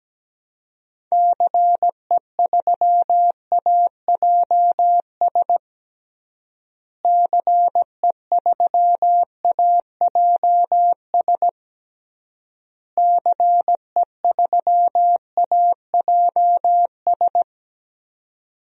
Morse code, heard as a keyed tone, about 17 wpm, 700 Hz. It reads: CE3AJS CE3AJS CE3AJS
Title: DXCC-100 Call Signs - 6 Characters - Repeated 3x - 17wpm